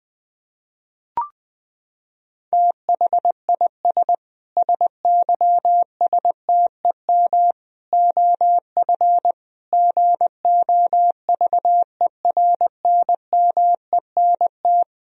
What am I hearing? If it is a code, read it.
Morse code, heard as a keyed tone, about 20 wpm, 700 Hz. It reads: THIS SYSTEM OF GOVERNMENT
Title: Sentences from Top 1000 Words - Review Part 3 - 20wpm